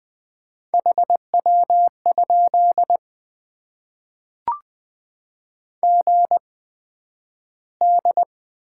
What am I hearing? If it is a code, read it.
Morse code, heard as a keyed tone, about 20 wpm, spaced as for 11 wpm, 700 Hz. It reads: HW? GD
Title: Top 90 QSO Elements - 8x Character Spacing - 20wpm